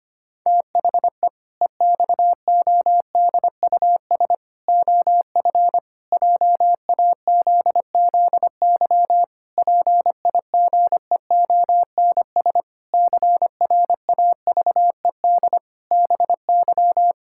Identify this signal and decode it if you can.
Morse code, heard as a keyed tone, about 25 wpm, 700 Hz. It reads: THE EXODUS OF JAZZY PIGEONS CRAVED BY
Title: Pangram Challenge v2 - 25wpm